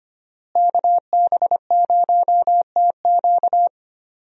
Morse code, heard as a keyed tone, about 25 wpm, 700 Hz. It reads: KB0TQ
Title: USA Call Signs - 5 Characters - Repeated 3x - 25wpm